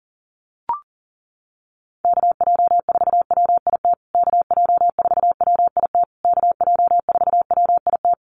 Morse code, heard as a keyed tone, about 40 wpm, 700 Hz. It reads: KJ4WIT KJ4WIT KJ4WIT
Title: USA Call Signs Repeated 3x 40wpm